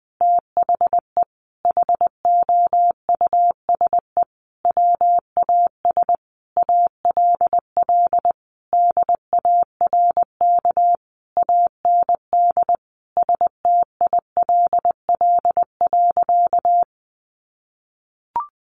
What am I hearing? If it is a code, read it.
Morse code, heard as a keyed tone, about 20 wpm, 700 Hz. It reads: THE HOUSE WAS ALL DARK AND STILL.
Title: Sentences from Top 500 Words 20wpm